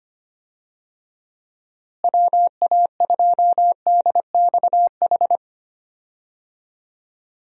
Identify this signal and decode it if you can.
Morse code, heard as a keyed tone, about 25 wpm, 700 Hz. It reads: WA2DXH
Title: USA Call Signs - 6 Characters - Repeated 2x - 25wpm